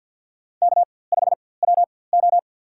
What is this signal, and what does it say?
Morse code, heard as a keyed tone, about 52 wpm, spaced as for 24 wpm, 690 Hz. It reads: K5WO
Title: USA Call Signs - 4x Character Spacing - 50wpm